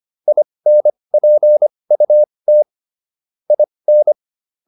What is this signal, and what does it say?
Morse code, heard as a keyed tone, about 25 wpm, spaced as for 20 wpm, 600 Hz. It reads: INPUT IN